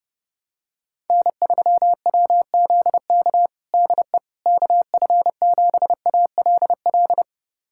Morse code, heard as a keyed tone, about 30 wpm, 700 Hz. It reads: N3WZK DE KF7ALL
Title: USA Callsign DE Callsign 30wpm